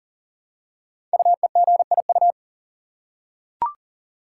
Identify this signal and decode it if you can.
Morse code, heard as a keyed tone, about 41 wpm, 700 Hz. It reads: UEGIU
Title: Code Groups - 5 Letters - 40wpm